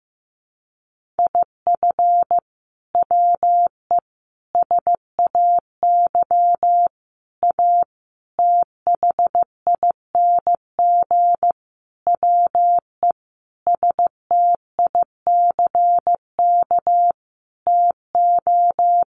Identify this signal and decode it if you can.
Morse code, heard as a keyed tone, about 15 wpm, 700 Hz. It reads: IF WE SAY A THING WE STICK TO
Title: Sentences from Top 1200 Words 15wpm